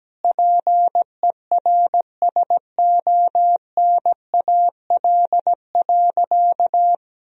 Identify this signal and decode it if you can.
Morse code, heard as a keyed tone, about 17 wpm, 700 Hz. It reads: PERSONAL.
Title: Sentences from Top 1000 Words - Review Part 4 - 17wpm